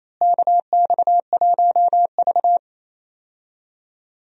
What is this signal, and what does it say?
Morse code, heard as a keyed tone, about 28 wpm, 700 Hz. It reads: KX1V
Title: USA Call Signs - 4 Characters - Repeated 3x - 28wpm